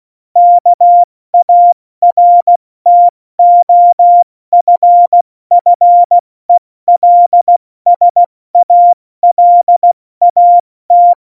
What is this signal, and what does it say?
Morse code, heard as a keyed tone, about 16 wpm, 700 Hz. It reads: KARTOFFELSALAT
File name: Morsekurs / Witze / Witz 03. November 2020, 70 F80